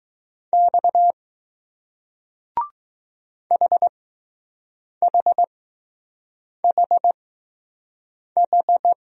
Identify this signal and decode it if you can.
Morse code, heard as a keyed tone, about 20 wpm, spaced as for 6 wpm, 700 Hz. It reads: X HHHH